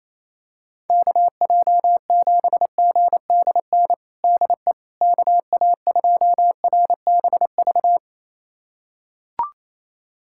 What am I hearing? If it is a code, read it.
Morse code, heard as a keyed tone, about 28 wpm, 700 Hz. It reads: KJ7GDN DE KA2RBV
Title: USA Callsign DE Callsign - 6 Characters - 28wpm